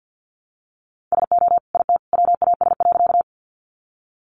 Morse code, heard as a keyed tone, about 50 wpm, 700 Hz. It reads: SO IT WAS.
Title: Sentences from Top 100 Words 50wpm